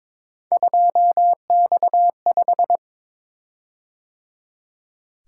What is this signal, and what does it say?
Morse code, heard as a keyed tone, about 22 wpm, 700 Hz. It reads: 2X5
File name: practice / 22wpm / Three Letter-Number Combinations - 22wpm